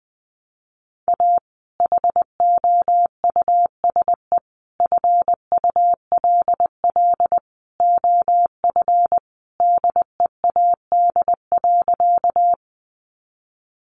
Morse code, heard as a keyed tone, about 20 wpm, 700 Hz. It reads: A HOUSE FULL OF DEAD.
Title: Sentences from Top 1500 Words 20wpm